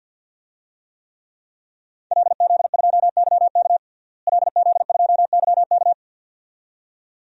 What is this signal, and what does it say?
Morse code, heard as a keyed tone, about 51 wpm, 700 Hz. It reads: LZ2YK LZ2YK
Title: DXCC-100 Call Signs - 5 Characters - Repeated 2x - 50wpm